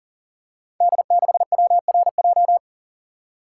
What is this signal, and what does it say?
Morse code, heard as a keyed tone, about 43 wpm, 700 Hz. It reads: D6WRJ
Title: Code Groups - 5 Letter:Numbers - 40wpm